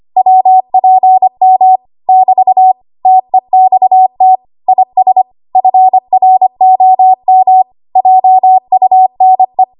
Morse code, heard as a keyed tone, about 25 wpm, 750 Hz. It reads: WPM <BT> TEXT IS FROM JUNE